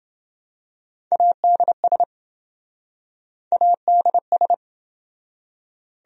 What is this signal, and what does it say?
Morse code, heard as a keyed tone, about 26 wpm, 700 Hz. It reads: ADS ADS